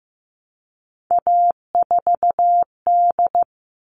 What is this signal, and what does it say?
Morse code, heard as a keyed tone, about 15 wpm, 700 Hz. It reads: A4D
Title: All Cut Numbers - 3-Digit 15wpm